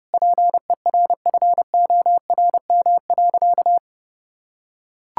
Morse code, heard as a keyed tone, about 30 wpm, 700 Hz. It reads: PERFORM.
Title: Sentences from Top 1000 Words - Review Part 6 - 30wpm